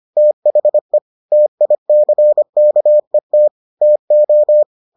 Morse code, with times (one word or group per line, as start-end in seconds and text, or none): 0.2-1.0 THE
1.3-3.5 TICKET
3.8-4.6 TO